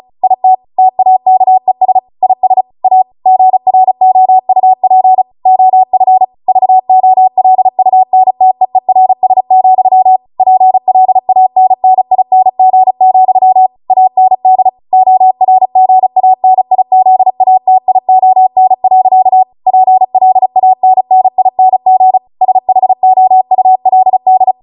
Morse code, heard as a keyed tone, about 36 wpm, 750 Hz. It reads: IT TAKES IS A GROUP OF VOLUNTEERS, PLANNING, AND ORGANIZATION. PLANNING SHOULD